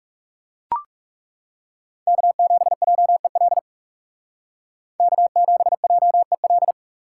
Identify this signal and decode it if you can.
Morse code, heard as a keyed tone, about 38 wpm, 690 Hz. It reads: K7JEL K7JEL